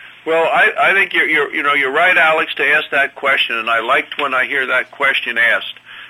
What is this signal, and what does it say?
Emergency Alert System (SAME) tones (header bursts), heard as harsh buzzing bursts, with speech. It is heard from 0.9 to 5.2 s.